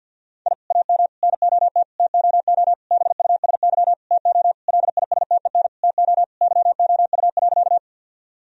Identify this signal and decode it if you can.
Morse code, heard as a keyed tone, about 50 wpm, 700 Hz. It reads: I AM NOT TOO BUSY TO LISTEN TO YOU.
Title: Sentences from Top 1000 Words - Review Part 3 - 50wpm